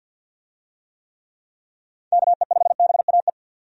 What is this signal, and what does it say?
Morse code, heard as a keyed tone, about 49 wpm, 700 Hz. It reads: KE5BAE
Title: USA Call Signs - 6 Characters - 50wpm